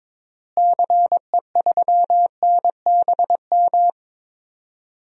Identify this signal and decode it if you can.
Morse code, heard as a keyed tone, about 22 wpm, 700 Hz. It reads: CE3NBM
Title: DXCC-100 Call Signs - 6 Characters - Repeated 3x - 22wpm